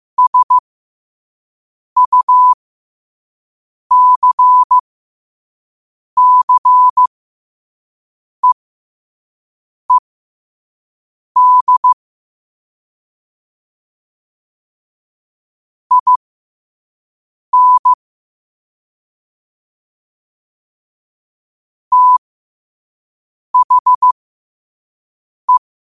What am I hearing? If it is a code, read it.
Morse code, heard as a keyed tone, about 15 wpm, spaced as for 5 wpm, 1000 Hz. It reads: SUCCEED IN THE
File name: qotdfiles / 5wpm